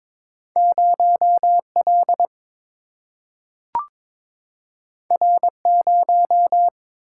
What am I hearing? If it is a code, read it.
Morse code, heard as a keyed tone, about 22 wpm, 700 Hz. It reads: 0 L R 0
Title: Single-Letter Single-Number 22wpm